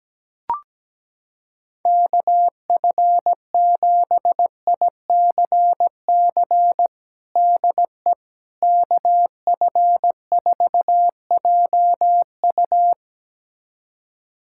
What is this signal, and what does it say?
Morse code, heard as a keyed tone, about 17 wpm, 700 Hz. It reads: KF7ICC DE KF4JU
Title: USA Callsign DE Callsign 17wpm